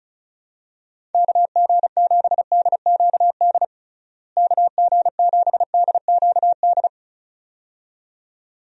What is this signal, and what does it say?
Morse code, heard as a keyed tone, about 34 wpm, 700 Hz. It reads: KG7DQD KG7DQD
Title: USA Call Signs - 6 Characters - Repeated 2x - 35wpm